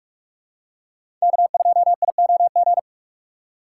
Morse code, heard as a keyed tone, about 46 wpm, 700 Hz. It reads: K2IOG